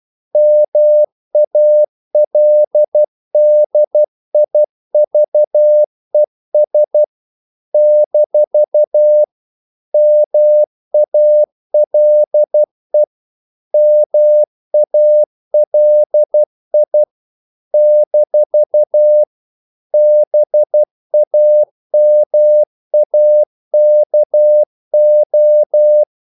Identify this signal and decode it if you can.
Morse code, heard as a keyed tone, about 12 wpm, 600 Hz. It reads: MALDIVES - MALE MALI - BAMAKO
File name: CW / Practice / NationalCapitals12